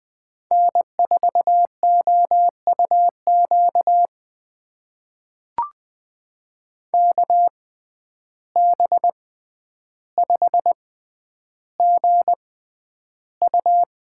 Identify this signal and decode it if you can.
Morse code, heard as a keyed tone, about 20 wpm, spaced as for 10 wpm, 700 Hz. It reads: N4OUQ KB5GU
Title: USA Call Signs - 5 Characters - 6x Character Spacing - 20wpm